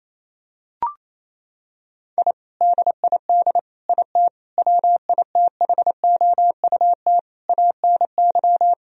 Morse code, heard as a keyed tone, about 27 wpm, 700 Hz. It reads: I DID IT WITHOUT ANY
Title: Sentences from Top 1000 Words - Review Part 3 - 28wpm